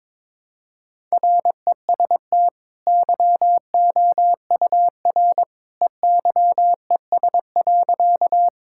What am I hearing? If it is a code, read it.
Morse code, heard as a keyed tone, about 22 wpm, 700 Hz. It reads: REST YOUR EYES.
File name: practice / 22wpm / Easy Sentences with Hints - 22wpm